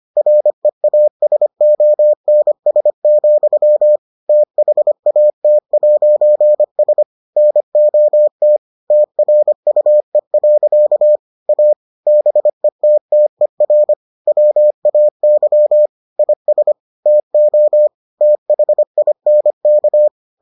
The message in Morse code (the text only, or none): REASONS, THAT'S NOT TRUE. A BETTER WAY IS TO THINK